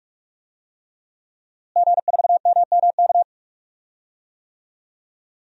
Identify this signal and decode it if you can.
Morse code, heard as a keyed tone, about 45 wpm, 700 Hz. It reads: G4MMK
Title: DXCC-100 Call Signs - 5 Characters - Repeated 3x - 45wpm